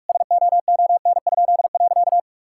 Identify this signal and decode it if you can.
Morse code, heard as a keyed tone, about 42 wpm, 700 Hz. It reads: SOON?.
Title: Pangram Challenge v2 - 45wpm